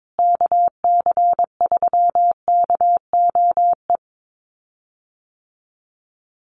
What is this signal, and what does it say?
Morse code, heard as a keyed tone, about 22 wpm, 700 Hz. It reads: KC3KOE
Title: USA Call Signs Repeated 3x 22wpm